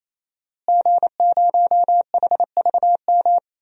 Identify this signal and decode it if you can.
Morse code, heard as a keyed tone, about 28 wpm, 700 Hz. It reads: G0HVM
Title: DXCC-100 Call Signs - 5 Characters - Repeated 3x - 28wpm